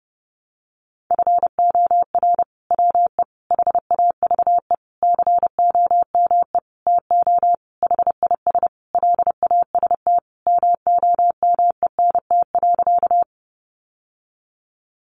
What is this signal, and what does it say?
Morse code, heard as a keyed tone, about 30 wpm, 700 Hz. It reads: FOR WE HAVE COME TO HIS LAST MOMENT.
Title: Sentences from Top 1000 Words 30wpm